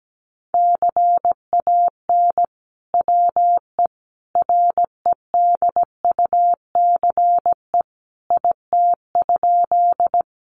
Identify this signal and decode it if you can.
Morse code, heard as a keyed tone, about 17 wpm, 700 Hz. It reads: CAN WE REDUCE IT?
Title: Sentences from Top 1100 Words 17wpm